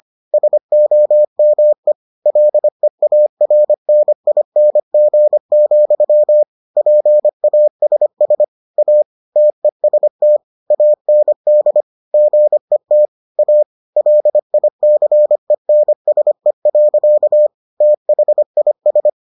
Morse code, heard as a keyed tone, about 25 wpm, 600 Hz. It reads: SOME LEARNING, PASS A TEST AND GET A LICENSE. THIS